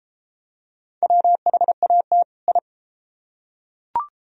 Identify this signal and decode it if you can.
Morse code, heard as a keyed tone, about 30 wpm, 700 Hz. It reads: WHAT I